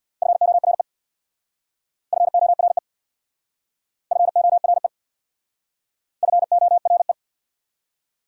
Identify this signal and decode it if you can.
Morse code, heard as a keyed tone, about 66 wpm, 690 Hz. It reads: FORE FORE FORE FORE